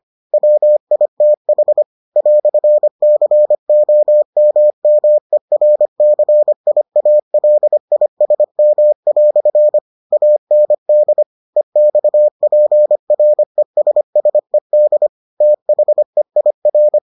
Morse code, heard as a keyed tone, about 25 wpm, 600 Hz. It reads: WITH "COMMERCIALISM" AND EXPRESSED THEIR